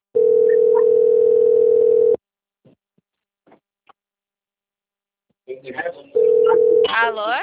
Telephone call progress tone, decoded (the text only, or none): ringback tone